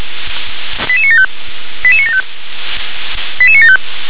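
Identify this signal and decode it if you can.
Five-tone selective call (ZVEI), heard as quick runs of five tones, tones of about 70 ms. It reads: 88975 88975 88975
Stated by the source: ZVEI five-tone selective call 88960 received via SDR.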